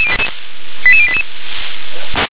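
Five-tone selective call (ZVEI), heard as quick runs of five tones, tones of about 70 ms. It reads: 88088
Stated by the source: ZVEI five-tone selective call received on SDR.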